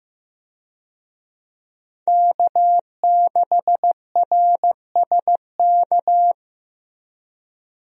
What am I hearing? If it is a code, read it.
Morse code, heard as a keyed tone, about 16 wpm, 700 Hz. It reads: K6RSK
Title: USA Call Signs - 5 Characters - Repeated 2x - 15wpm